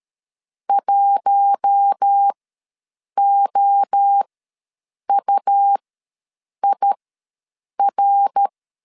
Morse code, heard as a keyed tone, about 13 wpm, spaced as for 7 wpm, 800 Hz. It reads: 1OUIR
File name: LEZIONI / 5 LBVC / 5 L ws 12wpm 2958